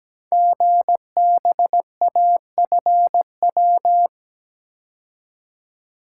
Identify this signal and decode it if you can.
Morse code, heard as a keyed tone, about 17 wpm, 700 Hz. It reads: GBAFW
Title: Code Groups - 5 Letters - 17wpm